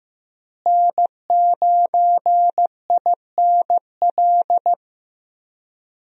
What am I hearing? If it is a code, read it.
Morse code, heard as a keyed tone, about 15 wpm, 700 Hz. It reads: N9INL
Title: USA Call Signs - 5 Characters - Repeated 3x - 15wpm